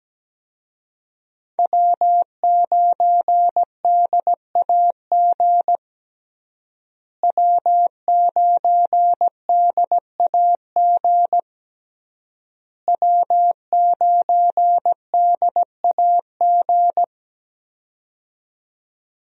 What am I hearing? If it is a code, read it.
Morse code, heard as a keyed tone, about 17 wpm, 700 Hz. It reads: W9DAG W9DAG W9DAG
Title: USA Call Signs - 5 Characters - Repeated 3x - 17wpm